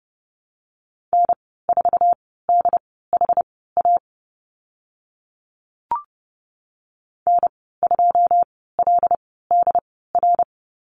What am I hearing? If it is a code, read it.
Morse code, heard as a keyed tone, about 30 wpm, spaced as for 17 wpm, 700 Hz. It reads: N4DHA N2LDR
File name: Lesson 15g - Number 5 - Call Signs 30wpm